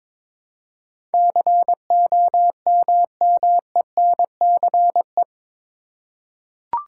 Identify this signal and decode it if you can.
Morse code, heard as a keyed tone, about 22 wpm, 700 Hz. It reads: COMMENCE
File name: practice / 22wpm / Eight Letter Words - 22wpm